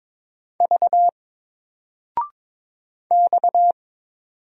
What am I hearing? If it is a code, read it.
Morse code, heard as a keyed tone, about 22 wpm, 700 Hz. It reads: V X
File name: Single-Letter-Number-22wpm